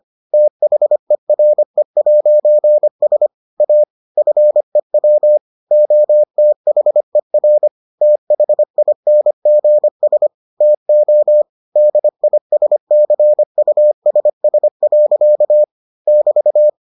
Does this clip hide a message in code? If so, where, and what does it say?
Yes: Morse code from 0.3 to 16.7 s, heard as a keyed tone, about 25 wpm, 600 Hz. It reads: THERE'S A FEW OTHER THINGS TO DISCUSS. <BT>